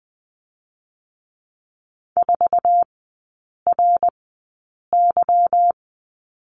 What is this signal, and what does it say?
Morse code, heard as a keyed tone, about 20 wpm, spaced as for 8 wpm, 700 Hz. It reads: 4RY
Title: 205 Lesson 33e - Letter j - Three Characters 20wpm